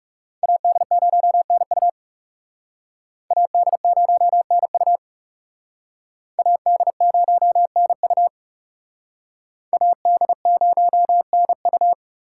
Morse code, heard as a keyed tone, about 37 wpm, 700 Hz. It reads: AD0NU AD0NU AD0NU AD0NU
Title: Speed Racing - USA 2x2 Call Signs 30wpm